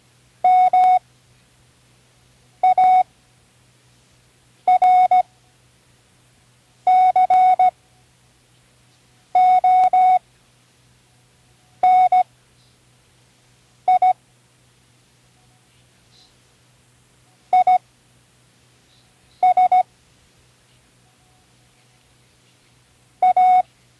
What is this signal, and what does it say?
Morse code, heard as a keyed tone, about 17 wpm, spaced as for 5 wpm, 740 Hz. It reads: MARCONI IS A
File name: CW Learning Programs / Learning Morse Code (from Tune in the World) / 08 words